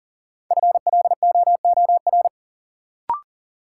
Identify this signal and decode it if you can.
Morse code, heard as a keyed tone, about 40 wpm, 700 Hz. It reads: FLOOR
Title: Sets of 3 Words - New Word Drill - 40wpm